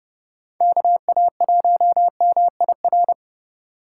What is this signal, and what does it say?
Morse code, heard as a keyed tone, about 29 wpm, 700 Hz. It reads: KA1MIR